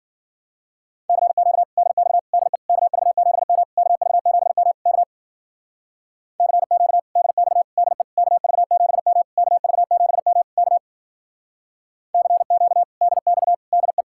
Morse code, heard as a keyed tone, about 47 wpm, 700 Hz. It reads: CQ DX DE KV7M KV7M K CQ DX DE KV7M KV7M K CQ DX DE
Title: Speed-Racing Common QSO Phrases - 40wpm